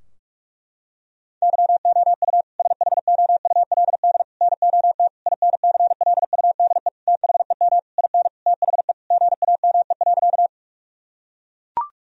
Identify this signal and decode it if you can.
Morse code, heard as a keyed tone, about 41 wpm, 700 Hz. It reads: YOU SHOULD NOT INCLUDE THEM IN THE GAME.
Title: Sentences from Top 1000 Words - Review Part 3 - 45wpm